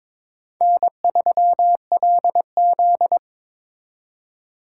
Morse code, heard as a keyed tone, about 26 wpm, 700 Hz. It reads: N3LZ
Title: Speed Racing - USA Call Signs 22wpm